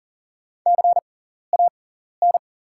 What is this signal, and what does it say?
Morse code, heard as a keyed tone, about 40 wpm, spaced as for 14 wpm, 700 Hz. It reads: CAN
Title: 6x Character Spacing - Top 100 Words - 40wpm